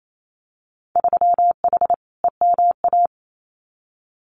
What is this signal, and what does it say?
Morse code, heard as a keyed tone, about 28 wpm, 700 Hz. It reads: 3H EMA
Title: Winter Field Day Exchanges Encore 28wpm